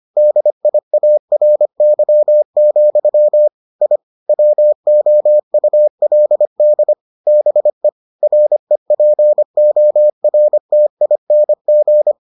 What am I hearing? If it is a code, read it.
Morse code, heard as a keyed tone, about 25 wpm, 600 Hz. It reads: DIARY, I WOULD BE REPORTING